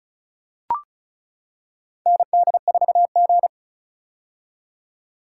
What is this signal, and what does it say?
Morse code, heard as a keyed tone, about 36 wpm, 700 Hz. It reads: ND4G
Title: K1USN SST QSO Practice - 35wpm